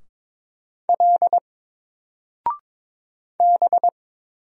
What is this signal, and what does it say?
Morse code, heard as a keyed tone, about 22 wpm, 700 Hz. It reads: L B
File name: Single-Letter-Number-22wpm